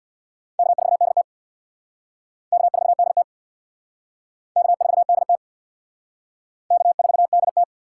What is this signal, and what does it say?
Morse code, heard as a keyed tone, about 59 wpm, 700 Hz. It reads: K4DT K4DT K4DT K4DT
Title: Speed Racing - USA 1x2 Call Signs 50wpm